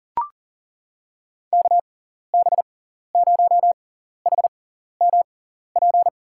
Morse code, heard as a keyed tone, about 40 wpm, spaced as for 14 wpm, 700 Hz. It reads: KB0HMP
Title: USA Call Signs - 6x Character Spacing - 40wpm